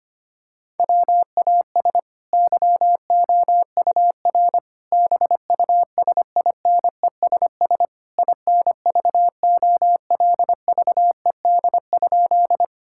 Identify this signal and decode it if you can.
Morse code, heard as a keyed tone, about 25 wpm, 700 Hz. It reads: WAS YOUR BUSINESS INVOLVED?
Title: Sentences from Top 1000 Words - Review Part 5 - 25wpm